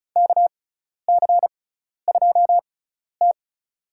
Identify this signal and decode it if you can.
Morse code, heard as a keyed tone, about 36 wpm, spaced as for 12 wpm, 700 Hz. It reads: KC2T
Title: USA Call Signs - 6 Characters - 6x Character Spacing - 35wpm